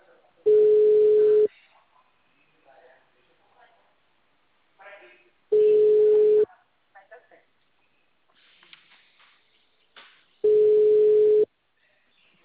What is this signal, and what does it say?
Ringback tone (Europe, 425 Hz), with speech.